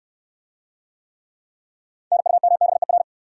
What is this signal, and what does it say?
Morse code, heard as a keyed tone, about 69 wpm, 700 Hz. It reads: NUMBER